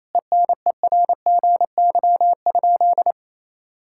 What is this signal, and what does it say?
Morse code, heard as a keyed tone, about 28 wpm, 700 Hz. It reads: ENERGY?
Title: Sentences from Top 1000 Words - Review Part 6 - 28wpm